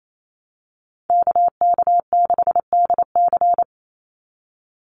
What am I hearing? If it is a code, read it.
Morse code, heard as a keyed tone, about 28 wpm, 700 Hz. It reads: KK6DC
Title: USA Call Signs 28wpm